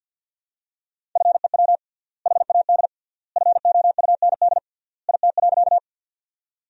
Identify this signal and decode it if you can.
Morse code, heard as a keyed tone, about 53 wpm, 700 Hz. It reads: FEW HAD FOUND IT.
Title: 3x Word-Spacing - Sentences from Top 300 Words - 50wpm